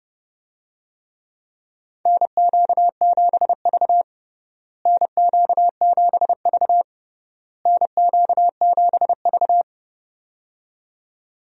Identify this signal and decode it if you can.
Morse code, heard as a keyed tone, about 30 wpm, 700 Hz. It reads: NQ7V NQ7V NQ7V
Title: USA Call Signs - 4 Characters - Repeated 3x - 30wpm